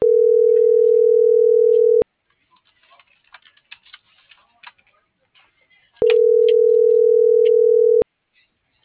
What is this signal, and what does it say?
Ringback tone (North America), 440 + 480 Hz.